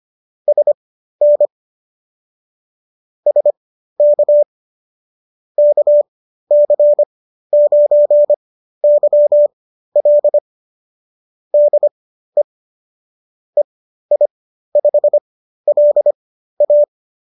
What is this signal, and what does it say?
Morse code, heard as a keyed tone, about 25 wpm, spaced as for 13 wpm, 590 Hz. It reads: SN SK KC9YL DE EI5LA